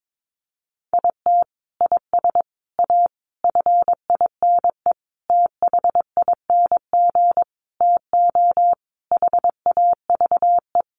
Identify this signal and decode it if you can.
Morse code, heard as a keyed tone, about 22 wpm, 700 Hz. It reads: IT IS A FINE THING TO HAVE
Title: Sentences from Top 1000 Words 22wpm